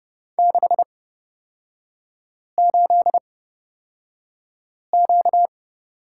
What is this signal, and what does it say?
Morse code, heard as a keyed tone, about 31 wpm, spaced as for 5 wpm, 700 Hz. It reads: 68Q
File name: Lesson 38e - Letter q - Three Characters 30wpm